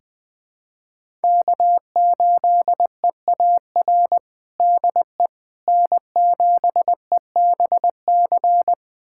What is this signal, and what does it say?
Morse code, heard as a keyed tone, about 21 wpm, 700 Hz. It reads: K8EAR DE N7EBC
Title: USA Callsign DE Callsign - 5 Characters - 20wpm